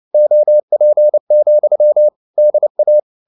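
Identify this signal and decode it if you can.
Morse code, heard as a keyed tone, about 29 wpm, 600 Hz. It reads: OP, DA
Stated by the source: Fast (QRQ) Morse at 29 wpm, from CW med Gnister.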